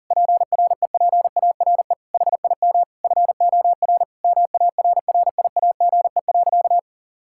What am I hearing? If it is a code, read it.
Morse code, heard as a keyed tone, about 40 wpm, 700 Hz. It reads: PREPARE HIM FOR MARRIAGE.
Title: Sentences from Top 1000 Words - Review Part 6 - 40wpm